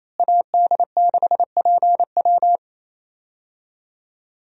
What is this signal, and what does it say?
Morse code, heard as a keyed tone, about 28 wpm, 700 Hz. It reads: AD6PW
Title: USA Call Signs - 5 Characters - Repeated 2x - 28wpm